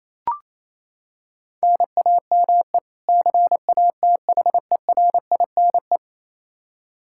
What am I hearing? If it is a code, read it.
Morse code, heard as a keyed tone, about 28 wpm, 700 Hz. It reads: NAME CATHERINE
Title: Practice QSO - Exchange 1 - Beginning - 28wpm